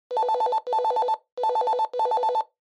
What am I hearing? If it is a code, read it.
There is no code.